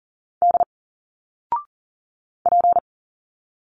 Morse code, heard as a keyed tone, about 40 wpm, 700 Hz. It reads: D P
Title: Single Letters 40wpm